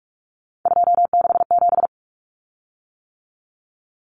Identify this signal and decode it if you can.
Morse code, heard as a keyed tone, about 44 wpm, 700 Hz. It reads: 267
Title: Common Cut Numbers - 3-Digit 45wpm